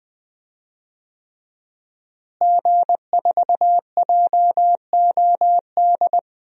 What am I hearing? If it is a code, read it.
Morse code, heard as a keyed tone, about 21 wpm, 700 Hz. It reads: G4JOD
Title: DXCC-100 Call Signs - 5 Characters - Repeated 3x - 20wpm